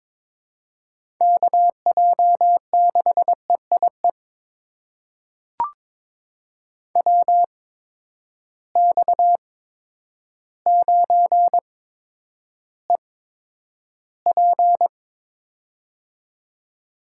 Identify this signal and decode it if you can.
Morse code, heard as a keyed tone, about 22 wpm, spaced as for 10 wpm, 700 Hz. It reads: KJ6EIE WX9EP
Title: USA Call Signs - 8x Character Spacing - 22wpm